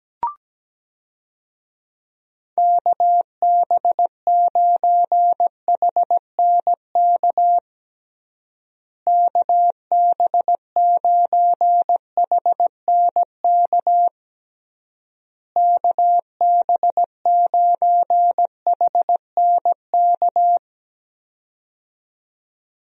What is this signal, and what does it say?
Morse code, heard as a keyed tone, about 17 wpm, 700 Hz. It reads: KB9HNK KB9HNK KB9HNK
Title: USA Call Signs - 6 Characters - Repeated 3x - 17wpm